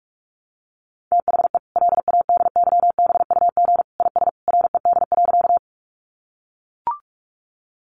Morse code, heard as a keyed tone, about 44 wpm, 700 Hz. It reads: THE LADYBUG IS RED.
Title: Additional Easy Phrases 45wpm